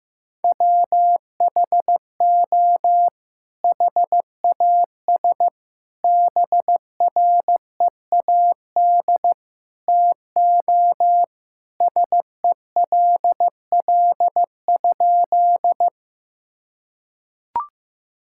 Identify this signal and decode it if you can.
Morse code, heard as a keyed tone, about 15 wpm, 700 Hz. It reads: WHO HAS BREAD TO SELL?
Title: Sentences from Top 1600 Words 15wpm